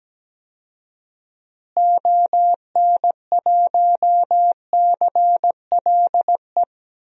Morse code, heard as a keyed tone, about 17 wpm, 700 Hz. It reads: ON1CLE
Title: DXCC-100 Call Signs - 6 Characters - Repeated 2x - 17wpm